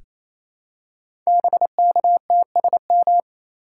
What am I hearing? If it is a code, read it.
Morse code, heard as a keyed tone, about 28 wpm, 700 Hz. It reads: BKTSM
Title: Code Groups - 5 Letters - 28wpm